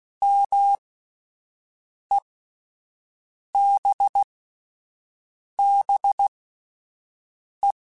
Morse code, heard as a keyed tone, about 16 wpm, spaced as for 5 wpm, 800 Hz. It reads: MEBBE
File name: (spaced) / QSOs Spaced-mp3s / WA7QOX 1 @10 sp